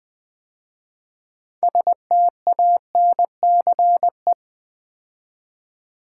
Morse code, heard as a keyed tone, about 20 wpm, 700 Hz. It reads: STANCE
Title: 6 Letter Words - First 3 Letters Given - 20wpm